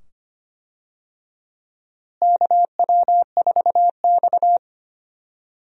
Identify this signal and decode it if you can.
Morse code, heard as a keyed tone, about 25 wpm, 700 Hz. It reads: KW4X
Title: USA Call Signs - 4 Characters - Repeated 2x - 25wpm